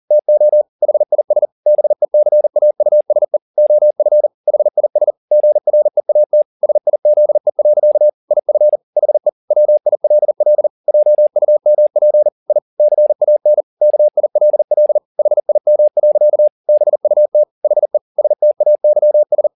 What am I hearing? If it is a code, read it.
Morse code, heard as a keyed tone, about 40 wpm, 600 Hz. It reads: TO HIS BECAUSE OF HIS GREAT SIZE. IF HE WILL JUMP I CAN KILL HIM. BUT HE STAYS